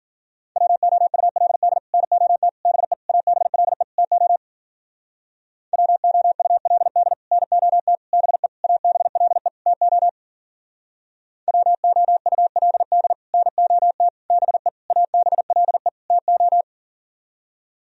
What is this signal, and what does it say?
Morse code, heard as a keyed tone, about 52 wpm, 700 Hz. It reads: WOULD NOT BE ABLE TO WOULD NOT BE ABLE TO WOULD NOT BE ABLE TO